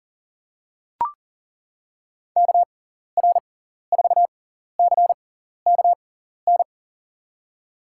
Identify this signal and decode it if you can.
Morse code, heard as a keyed tone, about 40 wpm, spaced as for 14 wpm, 700 Hz. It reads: KR4CKN
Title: USA Call Signs - 6 Characters - 6x Character Spacing - 40wpm